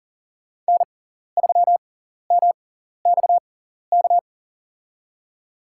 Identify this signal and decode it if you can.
Morse code, heard as a keyed tone, about 40 wpm, spaced as for 14 wpm, 700 Hz. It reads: N3MXK
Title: USA Call Signs - 5 Characters - 6x Character Spacing - 40wpm